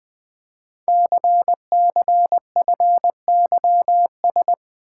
Morse code, heard as a keyed tone, about 20 wpm, 700 Hz. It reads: CCFYS
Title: Code Groups - 5 Letters - 20wpm